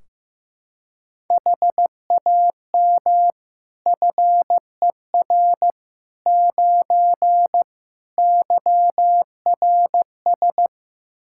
Morse code, heard as a keyed tone, about 15 wpm, 700 Hz. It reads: HAM FER 9 YRS
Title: K1USN SST QSO Practice 15wpm